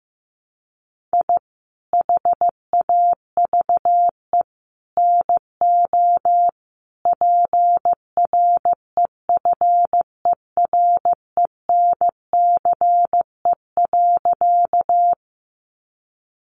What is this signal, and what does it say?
Morse code, heard as a keyed tone, about 15 wpm, 700 Hz. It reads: I HAVE NO PREFERENCE.
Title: Sentences from Top 1500 Words 15wpm